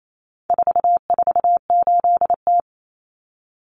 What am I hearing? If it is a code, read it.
Morse code, heard as a keyed tone, about 28 wpm, 700 Hz. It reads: 448T
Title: Common Cut Numbers - 4-Digit 28wpm